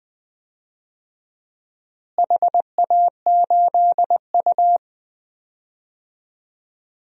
Morse code, heard as a keyed tone, about 20 wpm, 700 Hz. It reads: HA8U
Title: DXCC-100 Call Signs - 4 Characters - Repeated 2x - 20wpm